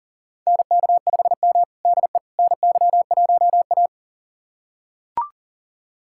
Morse code, heard as a keyed tone, about 40 wpm, 700 Hz. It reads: NK5M DE NY1A
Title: USA Callsign DE Callsign - 4 Characters - 40wpm